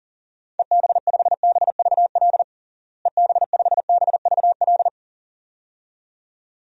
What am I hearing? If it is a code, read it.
Morse code, heard as a keyed tone, about 40 wpm, 700 Hz. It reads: EB5BVL EB5BVL
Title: DXCC-100 Call Signs - 6 Characters - Repeated 2x - 40wpm